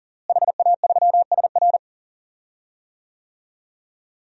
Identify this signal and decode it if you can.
Morse code, heard as a keyed tone, about 40 wpm, 700 Hz. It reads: HA3SR
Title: DXCC-100 Call Signs - 5 Characters - Repeated 3x - 40wpm